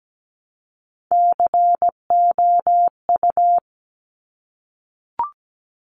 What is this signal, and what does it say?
Morse code, heard as a keyed tone, about 17 wpm, 700 Hz. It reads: COU
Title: Top Three Letter Combinations in Words 17wpm